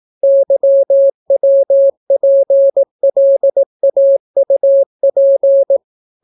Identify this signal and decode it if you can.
Morse code, heard as a keyed tone, about 18 wpm, 550 Hz. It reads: YWPLAUP